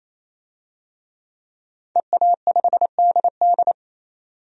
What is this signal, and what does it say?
Morse code, heard as a keyed tone, about 28 wpm, 690 Hz. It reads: EA5DD
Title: DXCC-100 Call Signs - 5 Characters - 28wpm